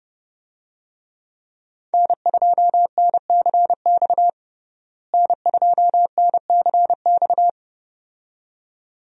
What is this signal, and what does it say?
Morse code, heard as a keyed tone, about 31 wpm, 700 Hz. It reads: N2NCX N2NCX
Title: USA Call Signs - 5 Characters - Repeated 2x - 30wpm